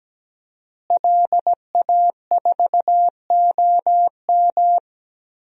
Morse code, heard as a keyed tone, about 17 wpm, 700 Hz. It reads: LA4OM
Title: DXCC-100 Call Signs - 5 Characters - Repeated 2x - 17wpm